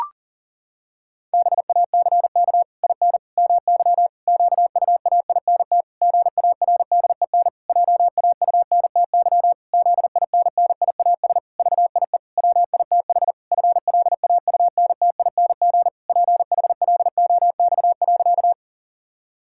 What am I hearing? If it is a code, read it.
Morse code, heard as a keyed tone, about 37 wpm, 700 Hz. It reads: BACK IN MY QUAINT GARDEN JAUNTY ZINNIAS VIE WITH FLAUNTING PHLOX.
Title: Pangram Challenge v2 - 40wpm